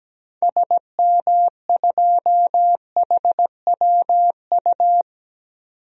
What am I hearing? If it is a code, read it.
Morse code, heard as a keyed tone, about 17 wpm, 700 Hz. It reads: SM2HWU
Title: DXCC-100 Call Signs - 6 Characters - 17wpm